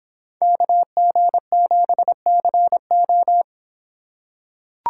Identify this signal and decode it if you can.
Morse code, heard as a keyed tone, about 26 wpm, 700 Hz. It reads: KG7CO